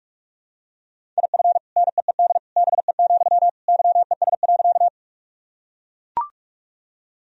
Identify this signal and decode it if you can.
Morse code, heard as a keyed tone, about 42 wpm, 700 Hz. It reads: IF NEED BE, YES.
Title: Sentences from Top 1000 Words - Review Part 1 - 45wpm